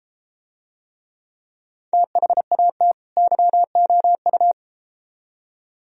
Morse code, heard as a keyed tone, about 32 wpm, 700 Hz. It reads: THAT YOU